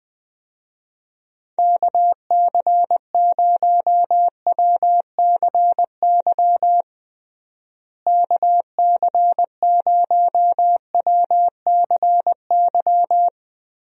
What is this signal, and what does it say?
Morse code, heard as a keyed tone, about 21 wpm, 700 Hz. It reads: KC0WCY KC0WCY
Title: USA Call Signs - 6 Characters - Repeated 3x - 20wpm